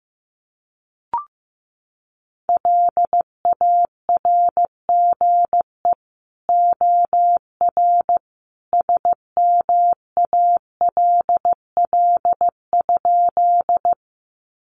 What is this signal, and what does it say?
Morse code, heard as a keyed tone, about 15 wpm, 700 Hz. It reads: LARGE OR SMALL?
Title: Sentences from Top 200 Words 15wpm